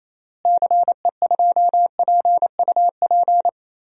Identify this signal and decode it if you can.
Morse code, heard as a keyed tone, about 28 wpm, 700 Hz. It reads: CE2PUP
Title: DXCC-100 Call Signs - 6 Characters - Repeated 3x - 28wpm